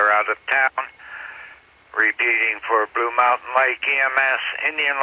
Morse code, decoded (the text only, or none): none